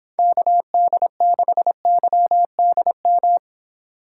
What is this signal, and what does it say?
Morse code, heard as a keyed tone, about 26 wpm, 700 Hz. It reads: KD6YDM